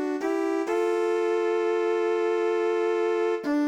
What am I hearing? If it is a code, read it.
Dial tone (North America), 350 + 440 Hz, with noise.